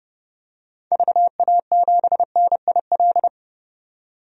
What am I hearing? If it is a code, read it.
Morse code, heard as a keyed tone, about 30 wpm, 700 Hz. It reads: VA7NIL
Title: DXCC-100 Call Signs - 6 Characters - 30wpm